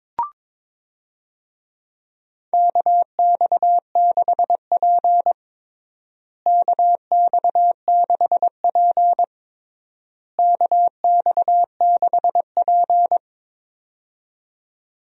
Morse code, heard as a keyed tone, about 22 wpm, 700 Hz. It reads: KX6P KX6P KX6P
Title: USA Call Signs - 4 Characters - Repeated 3x - 22wpm